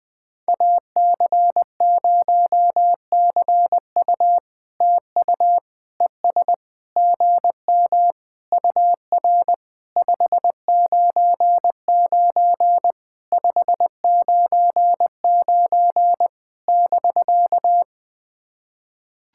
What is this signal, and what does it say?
Morse code, heard as a keyed tone, about 20 wpm, 660 Hz. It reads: AC0CU TU ES GM UR 599 599 <BK>
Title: POTA QSO Practice - 20wpm